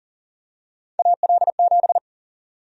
Morse code, heard as a keyed tone, about 40 wpm, 700 Hz. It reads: AL7
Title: USA Call Sign Prefixes - 40wpm